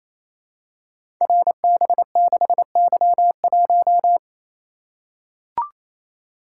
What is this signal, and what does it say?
Morse code, heard as a keyed tone, about 27 wpm, 700 Hz. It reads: RB6Y1
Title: Code Groups - 5 Letter:Numbers - 28wpm